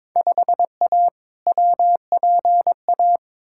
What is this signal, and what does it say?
Morse code, heard as a keyed tone, about 21 wpm, 700 Hz. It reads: 5A WPA
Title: Field Day Exchanges Encore - 22wpm